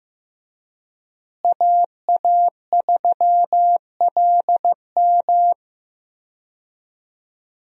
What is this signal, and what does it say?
Morse code, heard as a keyed tone, about 15 wpm, 700 Hz. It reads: AA3LM
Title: USA Call Signs - 5 Characters - Repeated 2x - 15wpm